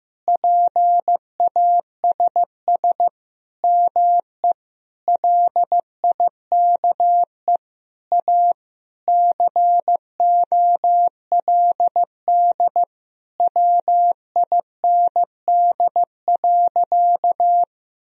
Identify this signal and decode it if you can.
Morse code, heard as a keyed tone, about 15 wpm, 700 Hz. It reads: PASS ME LIKE A COLD WIND.
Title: Sentences from Top 1000 Words - Review Part 2 - 15wpm